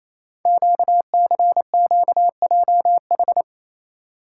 Morse code, heard as a keyed tone, about 28 wpm, 700 Hz. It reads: QCQJH
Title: Code Groups - 5 Letters - 28wpm